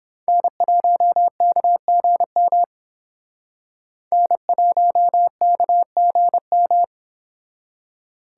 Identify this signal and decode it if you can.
Morse code, heard as a keyed tone, about 28 wpm, 700 Hz. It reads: N1KGM N1KGM